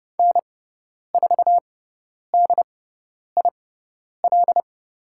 Morse code, heard as a keyed tone, about 30 wpm, spaced as for 10 wpm, 700 Hz. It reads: N4DIL
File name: Lesson 17g - Letter u - Call Signs 30wpm